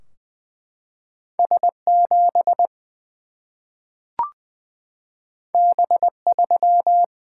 Morse code, heard as a keyed tone, about 20 wpm, 700 Hz. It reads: S 7 B 3
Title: Single-Letter Single-Number 20wpm